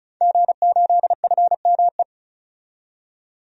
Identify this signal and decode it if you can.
Morse code, heard as a keyed tone, about 34 wpm, 700 Hz. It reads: G8FME
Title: DXCC-100 Call Signs - 5 Characters - 35wpm